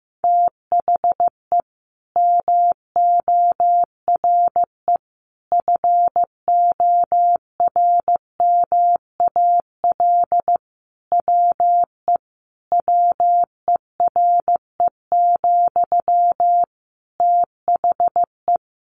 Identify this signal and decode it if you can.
Morse code, heard as a keyed tone, about 15 wpm, 700 Hz. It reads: THE MORE FORMAL WE WERE, THE
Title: Sentences from Top 1300 Words 15wpm